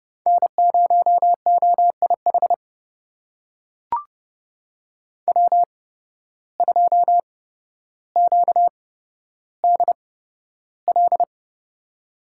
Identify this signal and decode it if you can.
Morse code, heard as a keyed tone, about 30 wpm, spaced as for 13 wpm, 700 Hz. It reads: N0OIH W2QDL
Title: USA Call Signs - 5 Characters - 8x Character Spacing - 30wpm